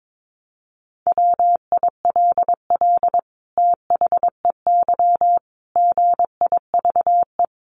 Morse code, heard as a keyed tone, about 22 wpm, 700 Hz. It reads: WILL THEY GIVE
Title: Sentences from Top 1400 Words 22wpm